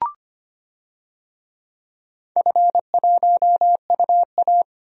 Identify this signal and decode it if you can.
Morse code, heard as a keyed tone, about 25 wpm, 700 Hz. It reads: F1UA
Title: DXCC-100 Call Signs - 4 Characters - 25wpm